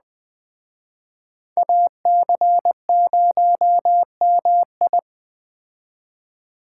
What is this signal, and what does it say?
Morse code, heard as a keyed tone, about 20 wpm, 700 Hz. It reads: AC0MI
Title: Speed Racing - USA 2x2 Call Signs 17wpm